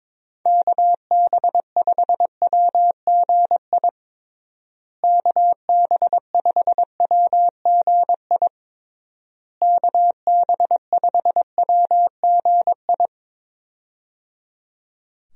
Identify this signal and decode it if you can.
Morse code, heard as a keyed tone, about 22 wpm, 700 Hz. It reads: KB5WGI KB5WGI KB5WGI
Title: USA Call Signs - 6 Characters - Repeated 3x - 22wpm